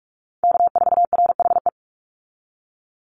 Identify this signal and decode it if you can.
Morse code, heard as a keyed tone, about 46 wpm, 700 Hz. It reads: K4RHE
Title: USA Call Signs 45wpm